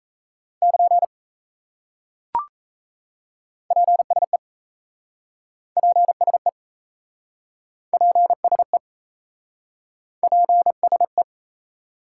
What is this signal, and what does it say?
Morse code, heard as a keyed tone, about 37 wpm, 700 Hz. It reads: <KN> PSE PSE PSE PSE